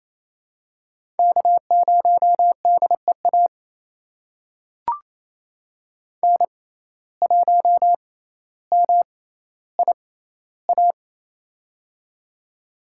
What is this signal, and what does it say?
Morse code, heard as a keyed tone, about 28 wpm, spaced as for 14 wpm, 700 Hz. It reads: K0DEA N1MIA
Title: USA Call Signs - 6x Character Spacing - 28wpm